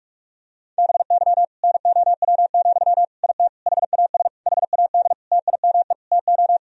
Morse code, heard as a keyed tone, about 46 wpm, 700 Hz. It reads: BY NOW, IT HAS HAD TIME TO
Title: Sentences from Top 1000 Words - Review Part 1 - 45wpm